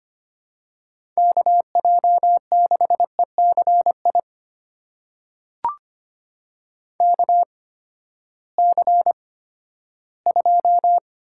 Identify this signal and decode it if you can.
Morse code, heard as a keyed tone, about 25 wpm, spaced as for 14 wpm, 700 Hz. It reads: KJ6ECI KC2
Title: USA Call Signs - 6 Characters - 8x Character Spacing - 25wpm